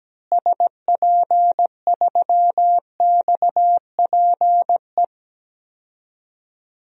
Morse code, heard as a keyed tone, about 17 wpm, 700 Hz. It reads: SP3XPE
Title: DXCC-100 Call Signs - 6 Characters - Repeated 2x - 17wpm